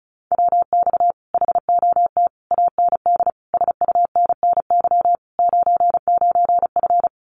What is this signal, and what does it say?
Morse code, heard as a keyed tone, about 34 wpm, 700 Hz. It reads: WX HOT AND SUNNY 99F
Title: Common QSO Phrases Encore 35wpm